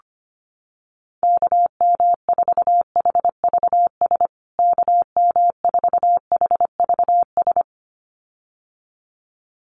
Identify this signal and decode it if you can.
Morse code, heard as a keyed tone, about 25 wpm, 700 Hz. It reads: KM4HVS KM4HVS
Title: USA Call Signs Repeated 2x 25wpm